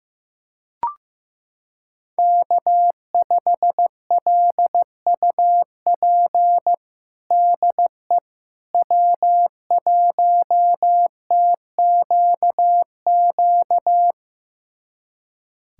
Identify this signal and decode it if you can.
Morse code, heard as a keyed tone, about 15 wpm, 700 Hz. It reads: K5LUP DE W1TQQ
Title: USA Callsign DE Callsign - 5 Characters - 15wpm